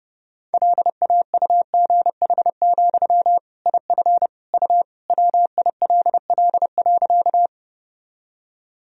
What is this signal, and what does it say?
Morse code, heard as a keyed tone, about 30 wpm, 700 Hz. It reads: LAUGH, IF U WILL.
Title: Sentences from Top 500 Words and Abbreviations 30wpm